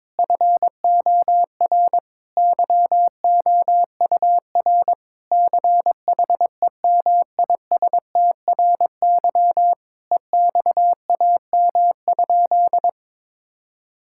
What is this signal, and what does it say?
Morse code, heard as a keyed tone, about 22 wpm, 700 Hz. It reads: FOR YOUR CHEMISTRY EXAM?
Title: Sentences from Top 1000 Words - Review Part 6 - 22wpm